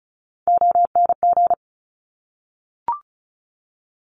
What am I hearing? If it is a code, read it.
Morse code, heard as a keyed tone, about 36 wpm, 700 Hz. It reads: ONG
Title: Top Three Letter Combinations in Words 35wpm